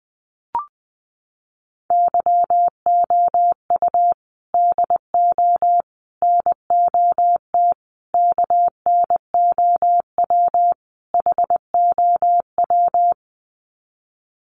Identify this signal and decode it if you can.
Morse code, heard as a keyed tone, about 20 wpm, 700 Hz. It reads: YOU DO NOT KNOW HOW
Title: Sets of 5 Words Encore 20wpm